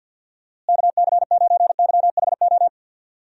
Morse code, heard as a keyed tone, about 50 wpm, 700 Hz. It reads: KC9YHO
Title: USA Call Signs - 6 Characters - Repeated 2x - 50wpm